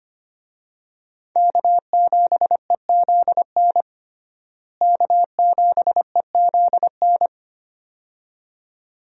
Morse code, heard as a keyed tone, about 25 wpm, 700 Hz. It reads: K7EZN K7EZN
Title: USA Call Signs - 5 Characters - Repeated 2x - 25wpm